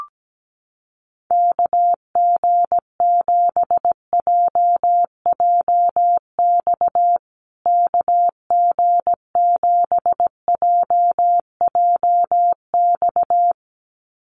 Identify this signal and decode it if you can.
Morse code, heard as a keyed tone, about 17 wpm, 700 Hz. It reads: KG7JJX KG7JJX
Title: USA Call Signs Repeated 2x 17wpm